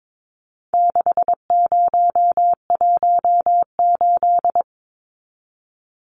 Morse code, heard as a keyed tone, about 22 wpm, 700 Hz. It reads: 6018
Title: Numbers - 4-Digit 22wpm